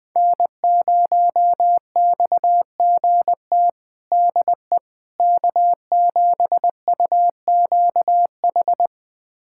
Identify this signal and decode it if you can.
Morse code, heard as a keyed tone, about 20 wpm, 700 Hz. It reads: N0XGT DE K7UQH
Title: USA Callsign DE Callsign - 5 Characters - 20wpm